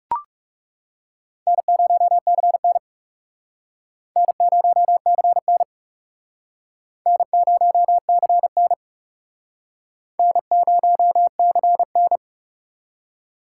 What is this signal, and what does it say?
Morse code, heard as a keyed tone, about 37 wpm, 700 Hz. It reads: N0CN N0CN N0CN N0CN
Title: Speed Racing - USA 1x2 Call Signs 30wpm